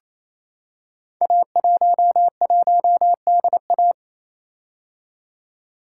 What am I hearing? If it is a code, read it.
Morse code, heard as a keyed tone, about 29 wpm, 700 Hz. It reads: A11DA
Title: Code Groups - 5 Letter:Numbers - 28wpm